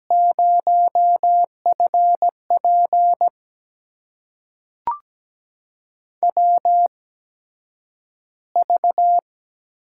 Morse code, heard as a keyed tone, about 17 wpm, spaced as for 9 wpm, 700 Hz. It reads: 0FP WV
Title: USA Call Signs - 4 Characters - 8x Character Spacing - 17wpm